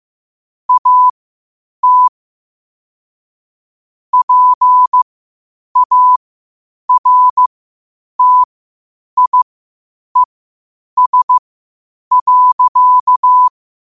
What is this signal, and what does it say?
Morse code, heard as a keyed tone, about 15 wpm, spaced as for 8 wpm, 1000 Hz. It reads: AT PARTIES.
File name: qotdfiles / 8wpm